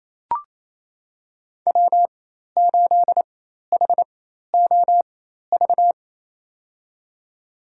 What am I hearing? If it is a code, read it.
Morse code, heard as a keyed tone, about 28 wpm, spaced as for 13 wpm, 690 Hz. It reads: W8HOV
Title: USA Call Signs - 5 Characters - 4x Character Spacing - 28wpm